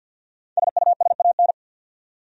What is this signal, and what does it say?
Morse code, heard as a keyed tone, about 48 wpm, 700 Hz. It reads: SUSAN
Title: USA Common Names - 50wpm